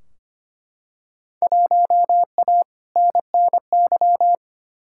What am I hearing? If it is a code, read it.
Morse code, heard as a keyed tone, about 25 wpm, 700 Hz. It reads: 1A NNY
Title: Field Day Exchanges Encore - 25wpm